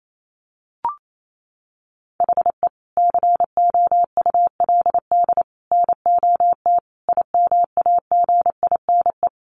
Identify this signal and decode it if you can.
Morse code, heard as a keyed tone, about 28 wpm, 700 Hz. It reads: HE COULD NOT IMAGINE
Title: Sentences from Top 1400 Words 28wpm